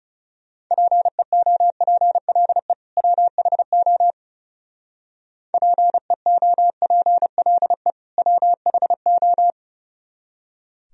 Morse code, heard as a keyed tone, about 31 wpm, 700 Hz. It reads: PEOPLE WHO PEOPLE WHO
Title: Speed-Racing - Sets of 2 Words 30wpm